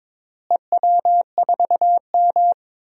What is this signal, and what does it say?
Morse code, heard as a keyed tone, about 22 wpm, 700 Hz. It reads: EW4M
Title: DXCC-100 Call Signs - 4 Characters - Repeated 3x - 22wpm